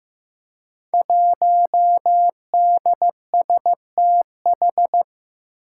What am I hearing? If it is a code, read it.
Morse code, heard as a keyed tone, about 15 wpm, 700 Hz. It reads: 1DSTH
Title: Five Challenging Letter-Number Combinations 15wpm